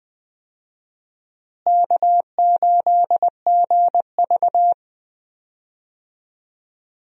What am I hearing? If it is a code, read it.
Morse code, heard as a keyed tone, about 20 wpm, 700 Hz. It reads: K8GV
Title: USA Call Signs - 4 Characters - Repeated 2x - 20wpm